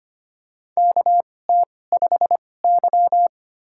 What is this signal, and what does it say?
Morse code, heard as a keyed tone, about 25 wpm, spaced as for 18 wpm, 700 Hz. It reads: KT5Y
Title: USA Call Signs - 4 Characters - 2x Character Spacing - 25wpm